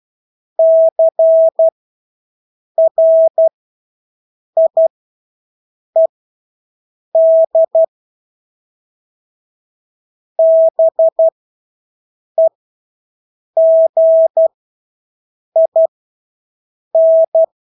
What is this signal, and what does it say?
Morse code, heard as a keyed tone, about 12 wpm, spaced as for 6 wpm, 650 Hz. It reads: CRIED BEGIN
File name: Engelske ord 12-6 wpm